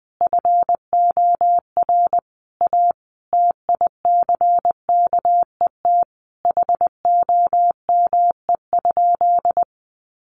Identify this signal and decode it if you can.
Morse code, heard as a keyed tone, about 20 wpm, 700 Hz. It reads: FOR A TICKET HOME?
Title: Sentences from Top 1400 Words 20wpm